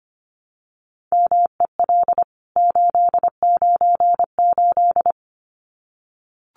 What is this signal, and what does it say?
Morse code, heard as a keyed tone, about 25 wpm, 700 Hz. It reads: MEL 898
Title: CWT Exchange 25wpm